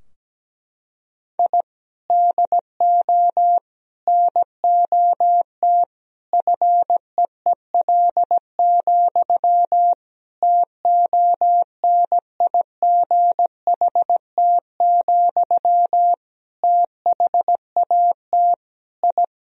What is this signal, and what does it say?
Morse code, heard as a keyed tone, about 17 wpm, 700 Hz. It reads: I DO NOT FEEL, TONIGHT, THAT I
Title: Sentences from Top 1000 Words - Review Part 3 - 17wpm